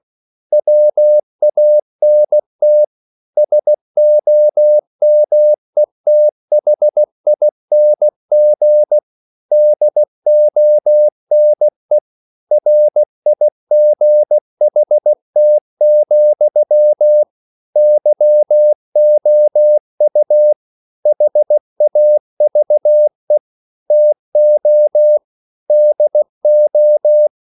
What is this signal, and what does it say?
Morse code, heard as a keyed tone, about 16 wpm, 600 Hz. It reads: WANT SOMETHING DONE RIGHT, YOU HAVE TO DO
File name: CW / Practice / EnglishSayings16